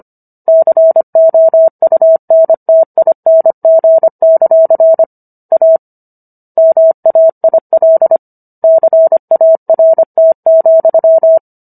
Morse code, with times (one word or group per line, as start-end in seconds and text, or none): 0.5-5.0 COUNTING;
5.5-5.8 A
6.6-8.2 MAIL
8.6-11.4 CART,